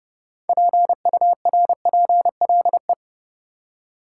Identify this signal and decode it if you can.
Morse code, heard as a keyed tone, about 29 wpm, 700 Hz. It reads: PURPLE
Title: Six Letter Words - 30wpm